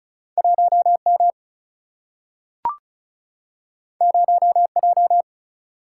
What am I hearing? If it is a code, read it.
Morse code, heard as a keyed tone, about 36 wpm, 700 Hz. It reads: 1M 0J
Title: Single-Letter Single-Number 35wpm